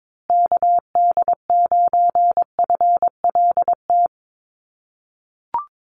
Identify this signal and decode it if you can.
Morse code, heard as a keyed tone, about 22 wpm, 700 Hz. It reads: KD9FLT
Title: USA Call Signs Repeated 3x 22wpm